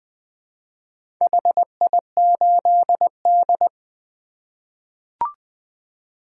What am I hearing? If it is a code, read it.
Morse code, heard as a keyed tone, about 20 wpm, 700 Hz. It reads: HI8D
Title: 2023 POTA Activator Call Signs - 20wpm